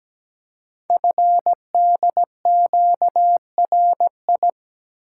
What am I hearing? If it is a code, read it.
Morse code, heard as a keyed tone, about 17 wpm, 700 Hz. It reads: FDQRI
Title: Code Groups - 5 Letters - 17wpm